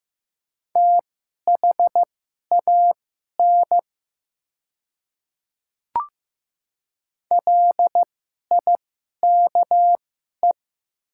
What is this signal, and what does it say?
Morse code, heard as a keyed tone, about 15 wpm, spaced as for 11 wpm, 700 Hz. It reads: THAN LIKE
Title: 2x Character Spacing - Top 100 Words - 15wpm